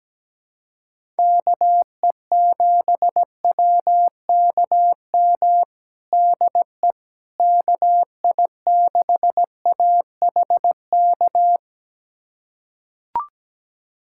Morse code, heard as a keyed tone, about 17 wpm, 700 Hz. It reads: KE7WKM DE KI6AHK
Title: USA Callsign DE Callsign - 6 Characters - 17wpm